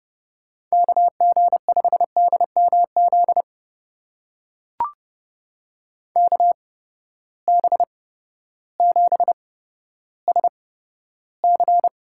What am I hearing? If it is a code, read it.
Morse code, heard as a keyed tone, about 30 wpm, spaced as for 14 wpm, 700 Hz. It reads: KG5DMZ KB7SC
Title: USA Call Signs - 6 Characters - 8x Character Spacing - 30wpm